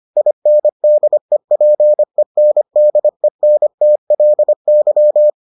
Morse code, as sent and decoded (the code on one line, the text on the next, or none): .. -. -.. .--. -. -.. -. - .-.. -.--
INDEPENDENTLY